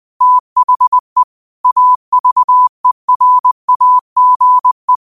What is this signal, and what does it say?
Morse code, heard as a keyed tone, about 20 wpm, 1000 Hz. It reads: THE AVERAGE